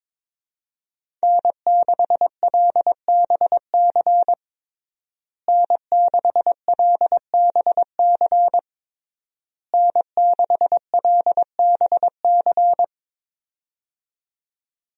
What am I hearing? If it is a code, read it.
Morse code, heard as a keyed tone, about 22 wpm, 700 Hz. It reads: N6LBC N6LBC N6LBC
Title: USA Call Signs - 5 Characters - Repeated 3x - 22wpm